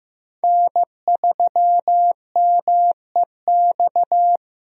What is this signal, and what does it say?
Morse code, heard as a keyed tone, about 15 wpm, 700 Hz. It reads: N3MEX
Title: USA Call Signs - 5 Characters - 15wpm